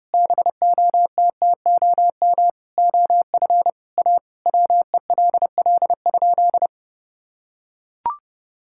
Morse code, heard as a keyed tone, about 30 wpm, 700 Hz. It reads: BOTTOM OF A WELL?
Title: Sentences from Top 1000 Words - Review Part 6 - 30wpm